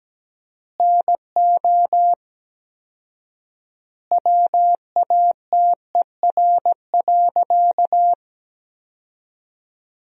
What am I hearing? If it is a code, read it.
Morse code, heard as a keyed tone, about 17 wpm, 700 Hz. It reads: NO WATER.
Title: 4x Word-Spacing - Sentences from Top 100 Words - 17wpm